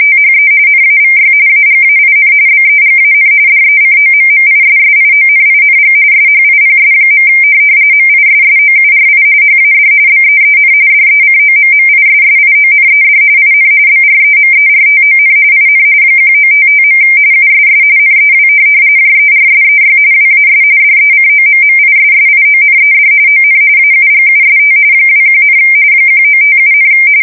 RTTY, about 45.45 baud. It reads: THE NAVY AGAIN BEGAN DECOMMISSIONING SHIPS. MISSOURI HAD BEEN DECOMMISSIONED IN BREMERTON IN 1955. USS NEW JERSEY WAS DECOMMISSIONED ON AUGUST 21, 1957,
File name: fsk170 / USS-New-Jersey